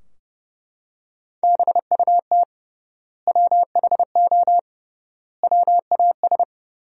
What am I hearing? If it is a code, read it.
Morse code, heard as a keyed tone, about 30 wpm, 700 Hz. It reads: BUT WHO WAS
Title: 3x Word-Spacing - Sentences from Top 200 Words - 30wpm